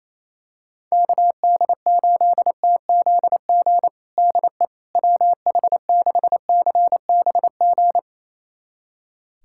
Morse code, heard as a keyed tone, about 28 wpm, 700 Hz. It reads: KD8TZG DE WH6CBG
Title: USA Callsign DE Callsign - 6 Characters - 28wpm